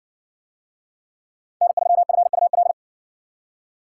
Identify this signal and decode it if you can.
Morse code, heard as a keyed tone, about 58 wpm, 700 Hz. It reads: N3FUL